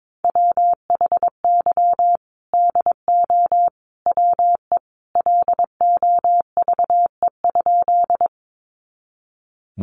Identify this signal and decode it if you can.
Morse code, heard as a keyed tone, about 22 wpm, 700 Hz. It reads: WHY DO WE LOVE?
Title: Sentences from Top 400 Words 22wpm